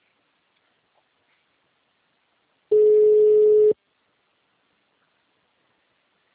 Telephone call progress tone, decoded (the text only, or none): ringback tone